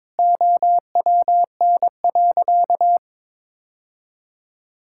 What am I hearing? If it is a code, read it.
Morse code, heard as a keyed tone, about 22 wpm, 700 Hz. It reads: OWN.
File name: practice / 22wpm / Sentences with 7-Word Phrases - 22wpm